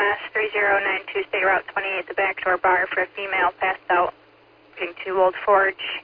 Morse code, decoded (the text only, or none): none